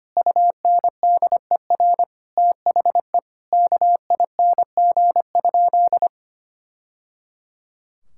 Morse code, heard as a keyed tone, about 25 wpm, 700 Hz. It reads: UNDER THE KING?
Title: Sentences from Top 1000 Words - Review Part 6 - 25wpm